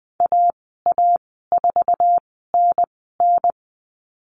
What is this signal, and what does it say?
Morse code, heard as a keyed tone, about 20 wpm, spaced as for 14 wpm, 700 Hz. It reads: AA4NN
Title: 062 Lesson 11g - Letter h - Call Signs 20wpm